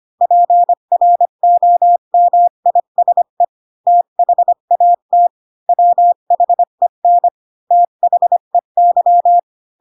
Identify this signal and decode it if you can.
Morse code, heard as a keyed tone, about 25 wpm, 700 Hz. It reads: PROMISE THAT WHEN THEY
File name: Pinocchio / Pinocchio ch8 20 25 ik1hge